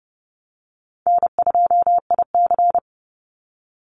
Morse code, heard as a keyed tone, about 30 wpm, 700 Hz. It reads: N2IC
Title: Easy Call Signs Repeated 2x - 30wpm